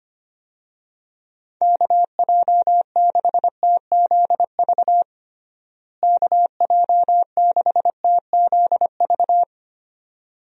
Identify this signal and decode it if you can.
Morse code, heard as a keyed tone, about 25 wpm, 700 Hz. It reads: KJ6TZV KJ6TZV
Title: USA Call Signs - 6 Characters - Repeated 2x - 25wpm